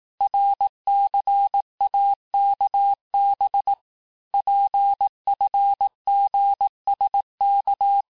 Morse code, heard as a keyed tone, about 18 wpm, 800 Hz. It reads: RCAKB PFGSK
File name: cours-du-lundi files / cours-1588-texte-chiffre-vitesse-17MPM